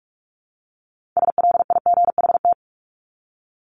Morse code, heard as a keyed tone, about 44 wpm, 700 Hz. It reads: SLIGHT
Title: LY Suffix - Top 100 Words with Base Words 45wpm